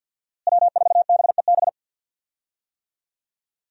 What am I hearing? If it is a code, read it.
Morse code, heard as a keyed tone, about 46 wpm, 700 Hz. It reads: W4BEB
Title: USA Call Signs - 5 Characters - Repeated 2x - 50wpm